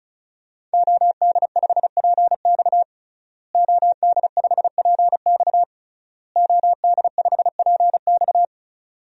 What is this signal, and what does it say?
Morse code, heard as a keyed tone, about 36 wpm, 700 Hz. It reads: OD5PX OD5PX OD5PX
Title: DXCC-100 Call Signs - 5 Characters - Repeated 3x - 35wpm